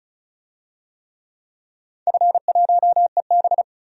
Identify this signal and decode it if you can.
Morse code, heard as a keyed tone, about 36 wpm, 690 Hz. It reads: F1EB
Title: DXCC-100 Call Signs - 4 Characters - 35wpm